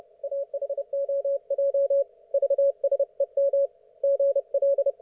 Morse code, heard as a keyed tone, about 30 wpm, 560 Hz. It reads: AHOJ VSEM GL
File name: Archiv / CUC179 / ok1fhi cul179